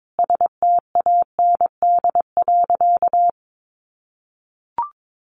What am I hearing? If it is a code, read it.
Morse code, heard as a keyed tone, about 22 wpm, 700 Hz. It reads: STAND.
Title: Additional Easy Phrases 22wpm